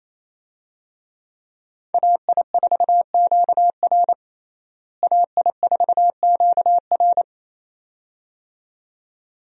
Morse code, heard as a keyed tone, about 28 wpm, 700 Hz. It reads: AI4QR AI4QR
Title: USA Call Signs - 5 Characters - Repeated 2x - 28wpm